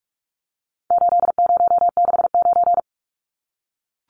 Morse code, heard as a keyed tone, about 44 wpm, 700 Hz. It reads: 8069
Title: Numbers - 4-Digit 45wpm